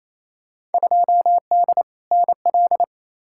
Morse code, heard as a keyed tone, about 28 wpm, 700 Hz. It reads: 2D NL
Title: Field Day Exchanges Encore - 28wpm